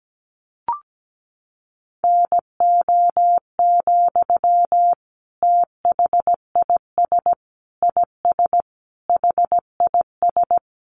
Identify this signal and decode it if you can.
Morse code, heard as a keyed tone, about 17 wpm, 700 Hz. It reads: NO, THIS IS HIS
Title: Sentences from Top 1100 Words 17wpm